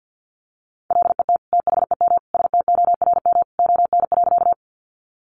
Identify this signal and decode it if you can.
Morse code, heard as a keyed tone, about 50 wpm, 700 Hz. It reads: LET THEM STORM ON.
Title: Sentences from Top 1500 Words 50wpm